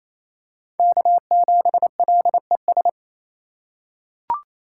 Morse code, heard as a keyed tone, about 28 wpm, 700 Hz. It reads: K7LES
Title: USA Call Signs - 5 Characters - 2x Character Spacing - 28wpm